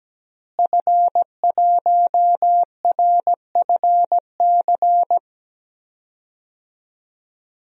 Morse code, heard as a keyed tone, about 17 wpm, 700 Hz. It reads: F1RFC
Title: DXCC-100 Call Signs - 5 Characters - Repeated 3x - 17wpm